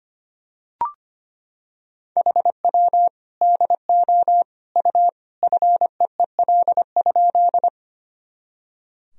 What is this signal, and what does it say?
Morse code, heard as a keyed tone, about 25 wpm, 700 Hz. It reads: HW DO U FEEL?
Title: Sentences from Top 400 Words and Abbreviations 25wpm